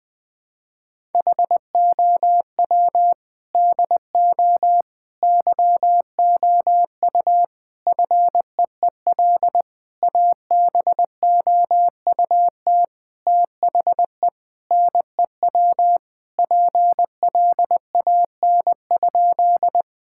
Morse code, heard as a keyed tone, about 20 wpm, 700 Hz. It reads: HOW DO YOU FEEL ABOUT THE NEW PLAN?
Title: Sentences with 5-Word Phrases - Spoken Beginning - 20wpm